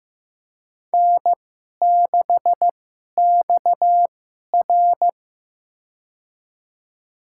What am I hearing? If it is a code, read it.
Morse code, heard as a keyed tone, about 15 wpm, spaced as for 11 wpm, 700 Hz. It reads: N6XR
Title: USA Call Signs - 2x Character Spacing - 15wpm